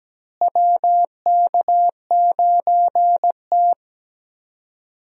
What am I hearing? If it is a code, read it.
Morse code, heard as a keyed tone, about 17 wpm, 700 Hz. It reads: WK9T
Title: USA Call Signs - 4 Characters - Repeated 2x - 17wpm